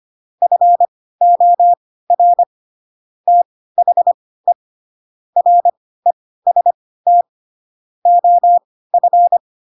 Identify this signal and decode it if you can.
Morse code, heard as a keyed tone, about 25 wpm, spaced as for 16 wpm, 700 Hz. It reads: FOR THE REST OF
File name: Pinocchio / Pinocchio ch2 10 25 ik1hge